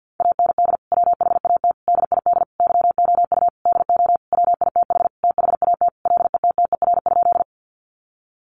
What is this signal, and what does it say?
Morse code, heard as a keyed tone, about 50 wpm, 700 Hz. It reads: AND WHAT DID YOU DO WITH THAT LETTER?
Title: Sentences from Top 300 Words 50wpm